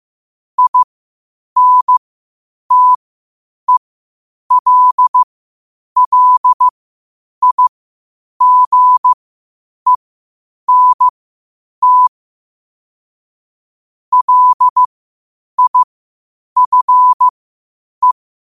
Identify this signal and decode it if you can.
Morse code, heard as a keyed tone, about 15 wpm, spaced as for 8 wpm, 1000 Hz. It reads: INTELLIGENT LIFE
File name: qotdfiles / 8wpm